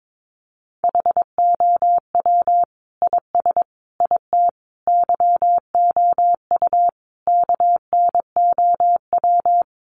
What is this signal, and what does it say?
Morse code, heard as a keyed tone, about 22 wpm, 700 Hz. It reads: HOW IS IT YOU KNOW
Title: Sentences from Top 700 Words 22wpm